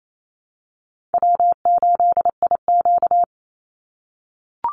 Morse code, heard as a keyed tone, about 28 wpm, 700 Hz. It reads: W8IQ
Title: Easy Call Signs - 28wpm